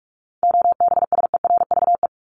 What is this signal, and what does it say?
Morse code, heard as a keyed tone, about 44 wpm, 700 Hz. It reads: OBSERVE
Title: ED Suffix - Top 100 Words with Base Words 45wpm